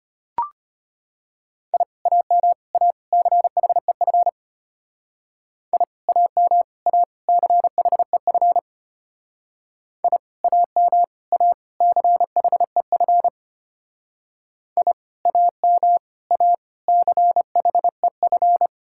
Morse code, heard as a keyed tone, about 31 wpm, 700 Hz. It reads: I AM A CHEF I AM A CHEF I AM A CHEF I AM A CHEF